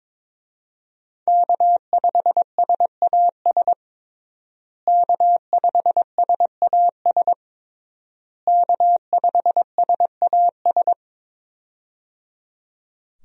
Morse code, heard as a keyed tone, about 22 wpm, 700 Hz. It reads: K5SAS K5SAS K5SAS
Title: USA Call Signs - 5 Characters - Repeated 3x - 22wpm